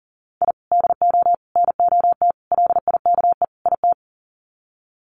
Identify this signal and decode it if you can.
Morse code, heard as a keyed tone, about 40 wpm, 700 Hz. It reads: I DO NOT LIKE IT
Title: Sets of 5 Words Encore 40wpm